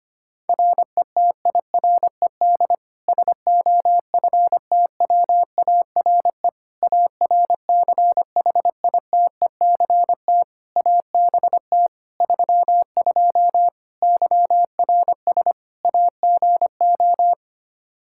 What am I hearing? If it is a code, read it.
Morse code, heard as a keyed tone, about 25 wpm, 700 Hz. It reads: RETIRED SOFTWARE ARCHITECT ABT 32 YRS AGO
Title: Practice QSO - Exchange 3 - More Info - 25wpm